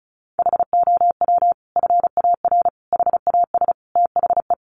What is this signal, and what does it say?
Morse code, heard as a keyed tone, about 34 wpm, 700 Hz. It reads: HOW FAR HAS THE
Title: Sentences from Top 1200 Words 35wpm